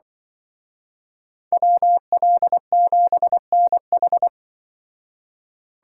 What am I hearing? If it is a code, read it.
Morse code, heard as a keyed tone, about 24 wpm, 700 Hz. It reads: WL7NH
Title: Speed Racing - USA Call Signs 20wpm